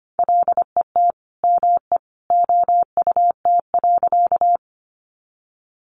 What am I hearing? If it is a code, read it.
Morse code, heard as a keyed tone, about 25 wpm, 700 Hz. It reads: LET ME OUT.
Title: Sentences from Top 300 Words 25wpm